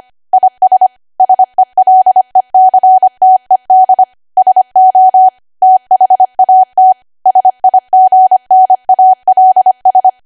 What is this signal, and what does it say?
Morse code, heard as a keyed tone, about 25 wpm, 750 Hz. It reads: IS SELECTED SO THAT SIGNALS
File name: Archive / 25 WPM / 25WPM